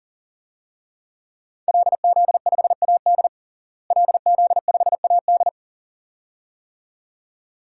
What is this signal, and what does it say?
Morse code, heard as a keyed tone, about 40 wpm, 700 Hz. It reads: LZ5AD LZ5AD
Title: DXCC-100 Call Signs - 5 Characters - Repeated 2x - 40wpm